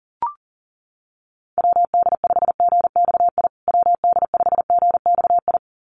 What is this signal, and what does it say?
Morse code, heard as a keyed tone, about 40 wpm, 700 Hz. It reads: WD5GXI WD5GXI
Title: USA Call Signs Repeated 2x 40wpm